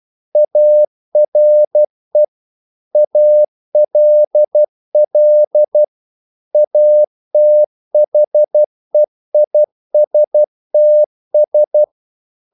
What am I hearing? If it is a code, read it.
Morse code, heard as a keyed tone, about 12 wpm, 600 Hz. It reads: ARE ALL ATHEISTS
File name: CW / Practice / EnglishQuotations0112